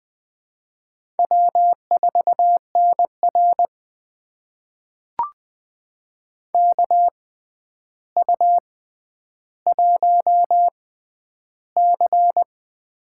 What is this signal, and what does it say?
Morse code, heard as a keyed tone, about 20 wpm, spaced as for 10 wpm, 700 Hz. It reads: W4NR KU1C
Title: USA Call Signs - 4 Characters - 6x Character Spacing - 20wpm